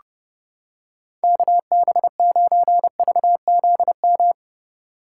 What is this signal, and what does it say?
Morse code, heard as a keyed tone, about 28 wpm, 700 Hz. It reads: KB9VZM